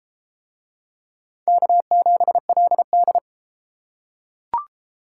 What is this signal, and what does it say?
Morse code, heard as a keyed tone, about 31 wpm, 700 Hz. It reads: K7LD